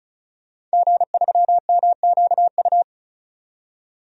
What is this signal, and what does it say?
Morse code, heard as a keyed tone, about 34 wpm, 700 Hz. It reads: G3MQU
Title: DXCC-100 Call Signs - 5 Characters - 35wpm